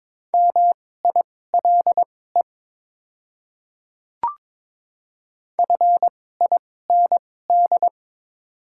Morse code, heard as a keyed tone, about 22 wpm, spaced as for 16 wpm, 700 Hz. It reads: MILE FIND
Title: 2x Character Spacing - Top 300 Words - 22wpm